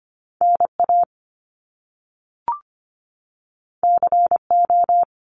Morse code, heard as a keyed tone, about 25 wpm, 700 Hz. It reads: NA CO
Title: Top Two Letter Combinations in Words 25wpm